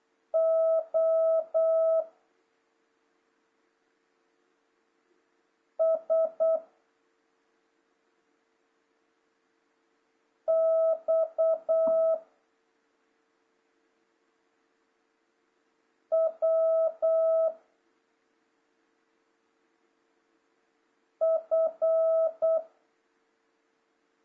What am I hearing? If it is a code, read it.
Morse code, heard as a keyed tone, about 8 wpm, 650 Hz. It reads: O S X W F
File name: jh8cbh / 2026cw